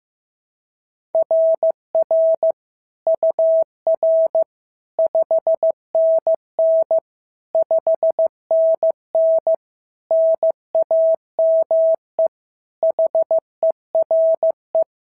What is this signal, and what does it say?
Morse code, heard as a keyed tone, about 15 wpm, 690 Hz. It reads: RR UR 5NN 5NN NAME HERE
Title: SOTA QSO Practice 15wpm